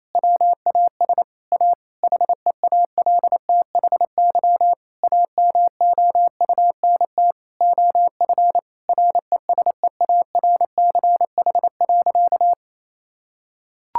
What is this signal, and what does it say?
Morse code, heard as a keyed tone, about 27 wpm, 700 Hz. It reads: WAS A HEALTHY AMOUNT OF RESEARCH.
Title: Sentences from Top 1000 Words - Review Part 5 - 28wpm